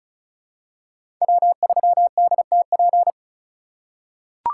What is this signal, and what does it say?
Morse code, heard as a keyed tone, about 36 wpm, 700 Hz. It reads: W3DTP
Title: USA Call Signs - 5 Characters - 4x Character Spacing - 35wpm